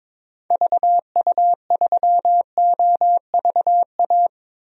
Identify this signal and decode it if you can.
Morse code, heard as a keyed tone, about 22 wpm, 700 Hz. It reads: VU3OVA
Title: DXCC-100 Call Signs - 6 Characters - 22wpm